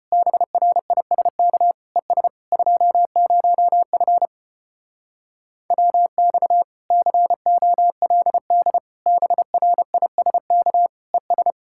Morse code, heard as a keyed tone, about 33 wpm, 700 Hz. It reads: BRISK ES 20F WX COLD BRISK ES